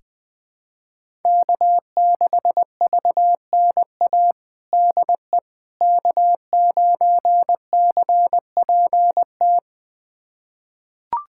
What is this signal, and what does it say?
Morse code, heard as a keyed tone, about 20 wpm, 700 Hz. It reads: K6VNA DE K9CPT
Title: USA Callsign DE Callsign - 5 Characters - 20wpm